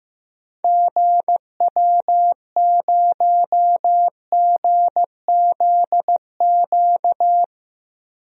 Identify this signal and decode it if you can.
Morse code, heard as a keyed tone, about 15 wpm, 700 Hz. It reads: GW0GZQ
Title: DXCC-100 Call Signs - 6 Characters - 15wpm